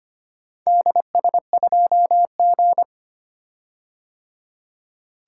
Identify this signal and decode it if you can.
Morse code, heard as a keyed tone, about 25 wpm, 700 Hz. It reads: DS2G
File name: practice / 25wpm / Four Letter-Number Combinations - 25wpm